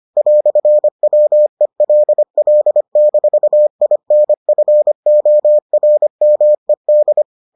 Morse code, heard as a keyed tone, about 25 wpm, 600 Hz. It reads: "WELL-INFORMED